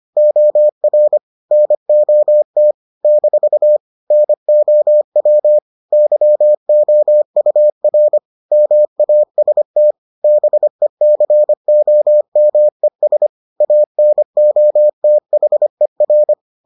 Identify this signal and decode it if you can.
Morse code, heard as a keyed tone, about 25 wpm, 600 Hz. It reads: OR NOT - NOW YOUR MAST BECOMES ANOTHER